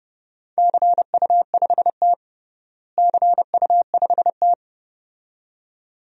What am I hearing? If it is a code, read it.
Morse code, heard as a keyed tone, about 32 wpm, 700 Hz. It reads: CU5T CU5T
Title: DXCC-100 Call Signs - 4 Characters - Repeated 2x - 30wpm